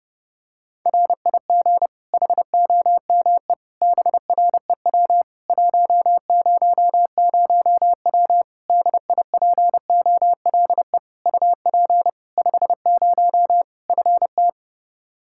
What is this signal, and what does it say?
Morse code, heard as a keyed tone, about 30 wpm, 700 Hz. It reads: RIG HOME BREW 100W DIPOLE UP 50 FT
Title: Practice QSO - Exchange 2 - Info - 30wpm